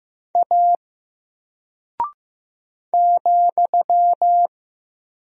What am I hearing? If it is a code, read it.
Morse code, heard as a keyed tone, about 15 wpm, 700 Hz. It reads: A ,
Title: Single Character (Letter:Number:Punctuation:Prosign) Alt 15wpm